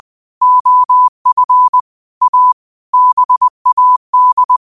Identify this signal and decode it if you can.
Morse code, heard as a keyed tone, about 20 wpm, 1000 Hz. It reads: OF A BAD